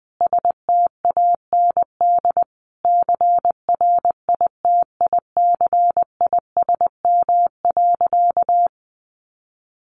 Morse code, heard as a keyed tone, about 20 wpm, 700 Hz. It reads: STAND CRITICISM.
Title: Sentences from Top 1300 Words 20wpm